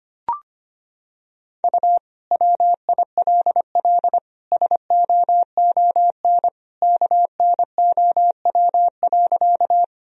Morse code, heard as a keyed tone, about 25 wpm, 700 Hz. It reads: U WILL SOON KNOW.
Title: Sentences from Top 400 Words and Abbreviations 25wpm